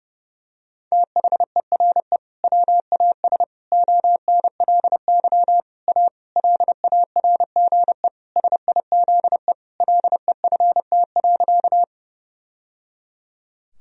Morse code, heard as a keyed tone, about 32 wpm, 700 Hz. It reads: THERE WAS ONLY A LARGE SIZE LEFT.
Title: Sentences from Top 1000 Words - Review Part 2 - 30wpm